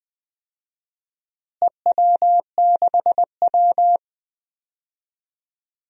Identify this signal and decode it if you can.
Morse code, heard as a keyed tone, about 20 wpm, 700 Hz. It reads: EW6W
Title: DXCC-100 Call Signs - 4 Characters - Repeated 2x - 20wpm